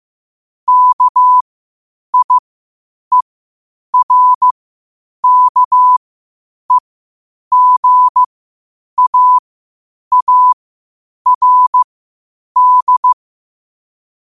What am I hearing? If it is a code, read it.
Morse code, heard as a keyed tone, about 15 wpm, spaced as for 8 wpm, 1000 Hz. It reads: KIERKEGAARD
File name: qotdfiles / 8wpm